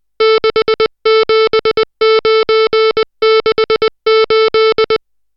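Morse code, heard as a keyed tone, about 20 wpm, 440 Hz, with noise. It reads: 67968
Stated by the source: Morse code drill of digits at 60 characters per minute.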